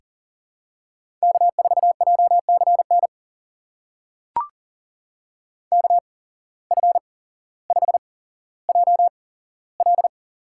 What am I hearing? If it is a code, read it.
Morse code, heard as a keyed tone, about 40 wpm, spaced as for 17 wpm, 700 Hz. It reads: K4JCN KF5JL
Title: USA Call Signs - 5 Characters - 8x Character Spacing - 40wpm